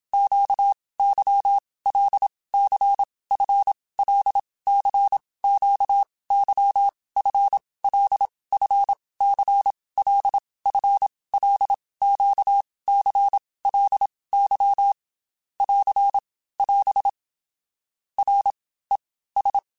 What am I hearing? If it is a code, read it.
Morse code, heard as a keyed tone, about 27 wpm, 800 Hz. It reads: Q Y L C F L C Q Y F L F C L F L Q C L Y <AR> <AS> R E S
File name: Basic Letter Groups By7Groups / LFCYQ / LFCYQ@25